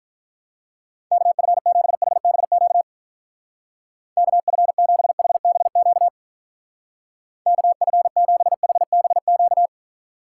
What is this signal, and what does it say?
Morse code, heard as a keyed tone, about 45 wpm, 700 Hz. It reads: KF7HBQ KF7HBQ KF7HBQ